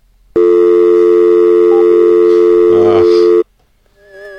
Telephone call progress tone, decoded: dial tone